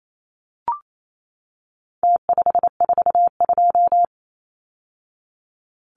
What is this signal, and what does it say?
Morse code, heard as a keyed tone, about 28 wpm, 700 Hz. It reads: T542
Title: Numbers - Contest Style - 4-Digit 28wpm